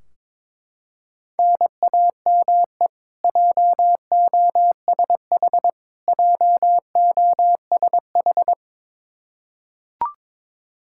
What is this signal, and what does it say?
Morse code, heard as a keyed tone, about 22 wpm, 700 Hz. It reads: NAME JOSH JOSH
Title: K1USN SST QSO Practice 22wpm